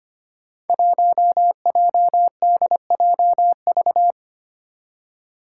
Morse code, heard as a keyed tone, about 25 wpm, 700 Hz. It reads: 1JDJV
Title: Five Challenging Letter-Number Combinations 25wpm